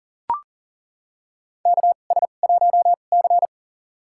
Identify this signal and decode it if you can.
Morse code, heard as a keyed tone, about 40 wpm, spaced as for 29 wpm, 700 Hz. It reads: KS1C
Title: USA Call Signs - 4 Characters - 2x Character Spacing - 40wpm